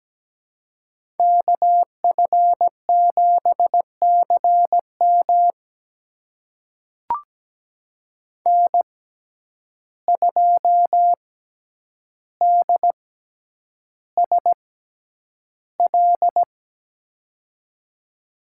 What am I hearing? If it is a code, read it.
Morse code, heard as a keyed tone, about 17 wpm, spaced as for 9 wpm, 700 Hz. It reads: KF7CM N2DSL
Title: USA Call Signs - 5 Characters - 6x Character Spacing - 17wpm